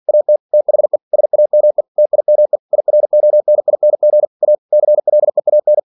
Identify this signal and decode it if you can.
Morse code, heard as a keyed tone, about 47 wpm, 600 Hz. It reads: AT THE SAME TIME IRONING A CLEAN
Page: Never 47wpm